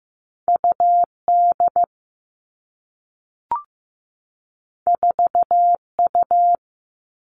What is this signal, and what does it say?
Morse code, heard as a keyed tone, about 15 wpm, 700 Hz. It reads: UD 4U
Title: All Cut Numbers - 2-Digit 15wpm